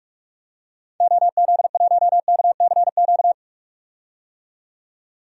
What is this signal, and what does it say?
Morse code, heard as a keyed tone, about 42 wpm, 700 Hz. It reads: OZ1KCQ
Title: DXCC-100 Call Signs - 6 Characters - 45wpm